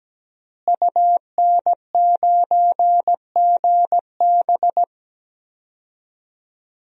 Morse code, heard as a keyed tone, about 17 wpm, 700 Hz. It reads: UN9GB
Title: DXCC-100 Call Signs - 5 Characters - Repeated 2x - 17wpm